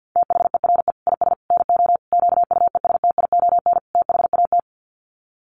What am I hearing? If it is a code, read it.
Morse code, heard as a keyed tone, about 50 wpm, 700 Hz. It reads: THERE IS NO QUESTION THAT
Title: Sets of 5 Words Encore 50wpm